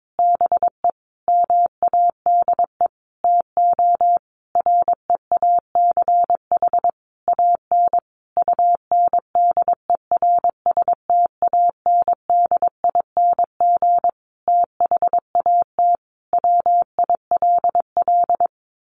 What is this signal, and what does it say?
Morse code, heard as a keyed tone, about 22 wpm, 700 Hz. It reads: BE MADE TO REACH AN UNDERSTANDING THAT WILL
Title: Sentences from Top 1500 Words 22wpm